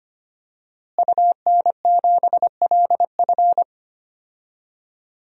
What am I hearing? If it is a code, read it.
Morse code, heard as a keyed tone, about 25 wpm, 700 Hz. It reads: UN7LF
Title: DXCC-100 Call Signs - 5 Characters - Repeated 3x - 25wpm